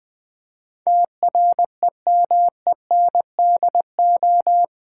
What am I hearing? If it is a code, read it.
Morse code, heard as a keyed tone, about 20 wpm, 700 Hz. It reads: TREMENDO
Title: Build A Word - 4 Letters At A Time - 10-Letter Words - 20wpm